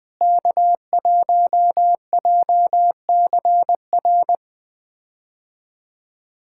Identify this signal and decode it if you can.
Morse code, heard as a keyed tone, about 20 wpm, 700 Hz. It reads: K1JCR
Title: K1USN SST QSO Practice - 20wpm